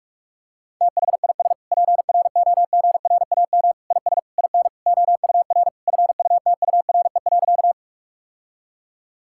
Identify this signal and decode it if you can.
Morse code, heard as a keyed tone, about 46 wpm, 700 Hz. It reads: THIS PROGRAM IS IN OUR FUTURE.
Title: Sentences from Top 1000 Words - Review Part 5 - 45wpm